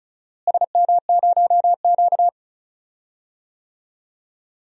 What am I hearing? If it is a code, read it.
Morse code, heard as a keyed tone, about 34 wpm, 700 Hz. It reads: SM0Q
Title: DXCC-100 Call Signs - 4 Characters - Repeated 3x - 35wpm